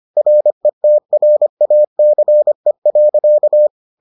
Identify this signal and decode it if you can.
Morse code, heard as a keyed tone, about 25 wpm, 600 Hz. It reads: RETRACE.